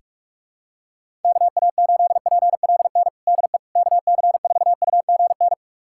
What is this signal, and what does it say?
Morse code, heard as a keyed tone, about 47 wpm, 700 Hz. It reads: KA8PLN DE KC4UGN
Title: USA Callsign DE Callsign 45wpm